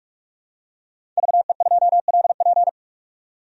Morse code, heard as a keyed tone, about 46 wpm, 700 Hz. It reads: VE2LP
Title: DXCC-100 Call Signs - 5 Characters - 45wpm